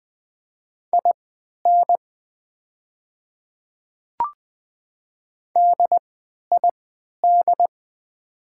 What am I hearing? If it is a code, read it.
Morse code, heard as a keyed tone, about 20 wpm, spaced as for 11 wpm, 700 Hz. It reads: IN DID
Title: 3x Character Spacing - Top 100 Words - 20wpm